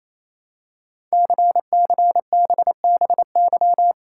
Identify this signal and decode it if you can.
Morse code, heard as a keyed tone, about 27 wpm, 700 Hz. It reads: CCBBY
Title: Code Groups - 5 Letters - 28wpm